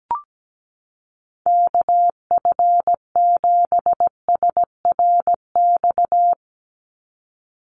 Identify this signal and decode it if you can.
Morse code, heard as a keyed tone, about 17 wpm, 700 Hz. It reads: KF7SRX
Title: USA Call Signs 17wpm